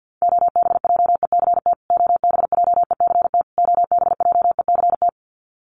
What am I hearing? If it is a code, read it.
Morse code, heard as a keyed tone, about 50 wpm, 700 Hz. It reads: OBJECT OBJECT OBJECT
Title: Top 500 Words Repeated 50wpm